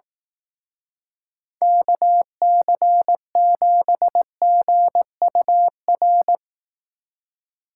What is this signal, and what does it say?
Morse code, heard as a keyed tone, about 18 wpm, 700 Hz. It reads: KC7GUR
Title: Speed Racing - USA 2x3 Call Signs 15wpm